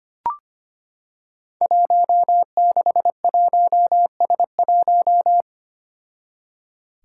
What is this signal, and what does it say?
Morse code, heard as a keyed tone, about 25 wpm, 700 Hz. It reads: 161S1
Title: Five Challenging Letter-Number Combinations 25wpm